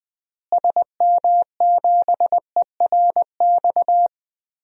Morse code, heard as a keyed tone, about 20 wpm, 700 Hz. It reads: SM7ERX
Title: DXCC-100 Call Signs - 6 Characters - Repeated 3x - 20wpm